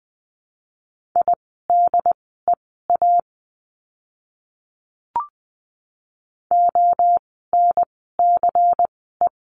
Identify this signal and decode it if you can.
Morse code, heard as a keyed tone, about 20 wpm, spaced as for 14 wpm, 700 Hz. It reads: IDEA ONCE
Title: 108 Lesson 18f - Letter m - Words 20wpm